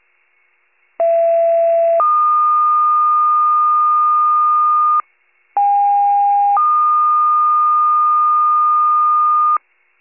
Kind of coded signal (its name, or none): two-tone page